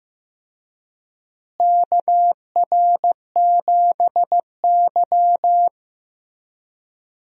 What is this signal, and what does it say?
Morse code, heard as a keyed tone, about 15 wpm, 700 Hz. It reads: KR7Y
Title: USA Call Signs - 4 Characters - Repeated 3x - 15wpm